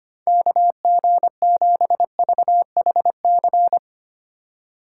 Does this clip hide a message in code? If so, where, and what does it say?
Yes: Morse code from 0.3 to 3.8 s, heard as a keyed tone, about 32 wpm, 700 Hz. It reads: KG7VHC